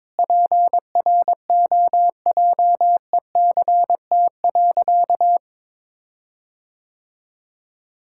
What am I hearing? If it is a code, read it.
Morse code, heard as a keyed tone, about 22 wpm, 700 Hz. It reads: PROJECT.
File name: Sentences with 7-Word Phrases - 22wpm